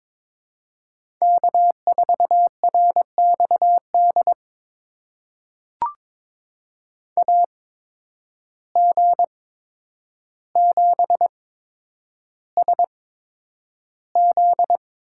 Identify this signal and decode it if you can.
Morse code, heard as a keyed tone, about 22 wpm, spaced as for 9 wpm, 700 Hz. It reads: K4RXD AG7SZ
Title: USA Call Signs - 5 Characters - 8x Character Spacing - 22wpm